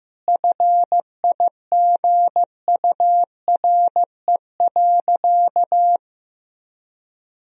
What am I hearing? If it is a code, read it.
Morse code, heard as a keyed tone, about 15 wpm, 700 Hz. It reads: FIGURE.
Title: Sentences from Top 1000 Words - Review Part 2 - 15wpm